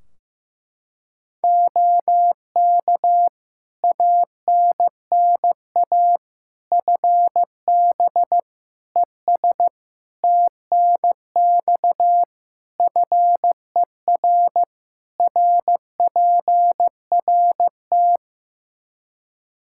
Morse code, heard as a keyed tone, about 16 wpm, 700 Hz. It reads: OK ANNA FB ES TNX FER RPRT
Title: Practice QSO - Exchange 2 - Info - 15wpm